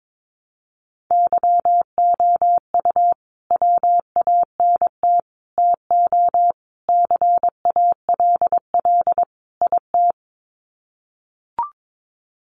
Morse code, heard as a keyed tone, about 22 wpm, 700 Hz. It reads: YOU WANT TO CALL IT
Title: Sets of 5 Words Encore2 22wpm